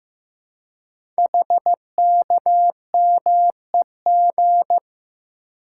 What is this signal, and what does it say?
Morse code, heard as a keyed tone, about 15 wpm, 700 Hz. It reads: HKMEG
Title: Code Groups - 5 Letters - 15wpm